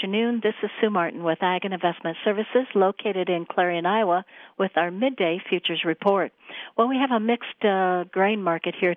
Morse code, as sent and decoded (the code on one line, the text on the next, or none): none